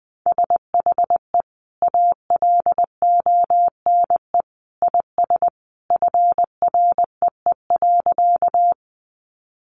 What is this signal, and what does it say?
Morse code, heard as a keyed tone, about 20 wpm, 700 Hz. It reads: SHE ALONE IS FREE.
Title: Sentences from Top 1000 Words 20wpm